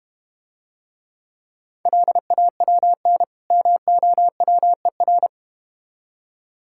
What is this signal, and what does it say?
Morse code, heard as a keyed tone, about 31 wpm, 700 Hz. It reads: LAWN MOWER
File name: practice / 25wpm / Difficult Words - Speed-Racing with Hints - 25wpm